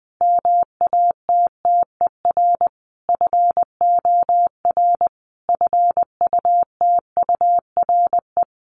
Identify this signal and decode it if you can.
Morse code, heard as a keyed tone, about 20 wpm, 700 Hz. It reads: MATTER FOR FUTURE
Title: Sentences from Top 1500 Words 20wpm